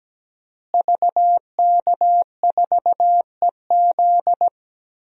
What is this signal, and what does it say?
Morse code, heard as a keyed tone, about 17 wpm, 700 Hz. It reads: VK4EZ
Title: DXCC-100 Call Signs - 5 Characters - Repeated 3x - 17wpm